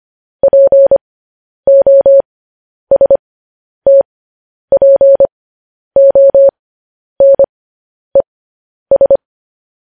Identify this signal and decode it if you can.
Morse code, heard as a keyed tone, about 25 wpm, spaced as for 10 wpm, 550 Hz. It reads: POSTPONES
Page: news-headlines-in-morse-code-at-10-wpm